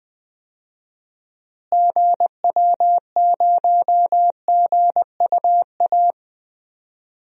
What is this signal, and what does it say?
Morse code, heard as a keyed tone, about 20 wpm, 700 Hz. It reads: GW0GUA
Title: DXCC-100 Call Signs - 6 Characters - Repeated 2x - 20wpm